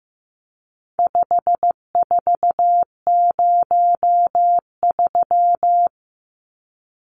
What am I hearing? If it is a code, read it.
Morse code, heard as a keyed tone, about 15 wpm, 700 Hz. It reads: 5403
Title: Numbers - 4-Digit 15wpm